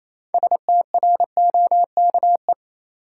Morse code, heard as a keyed tone, about 28 wpm, 700 Hz. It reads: STROKE
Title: Six Letter Words - 28wpm